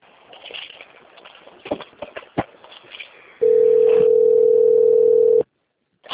Ringback tone (North America), 440 + 480 Hz.